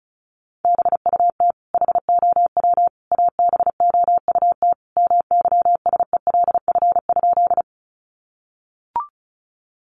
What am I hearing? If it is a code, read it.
Morse code, heard as a keyed tone, about 34 wpm, 700 Hz. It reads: BUT HOW ABOUT MYSELF?
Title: Sentences from Top 1300 Words 35wpm